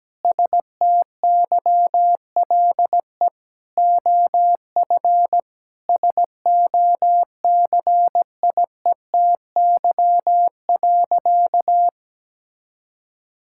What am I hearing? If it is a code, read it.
Morse code, heard as a keyed tone, about 17 wpm, 700 Hz. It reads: STYLE OF SOCIETY.
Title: Sentences from Top 1000 Words - Review Part 5 - 17wpm